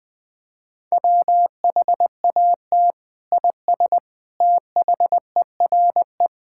Morse code, heard as a keyed tone, about 21 wpm, 700 Hz. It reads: WHAT IS THERE
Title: Sentences from Top 1000 Words - Review Part 5 - 20wpm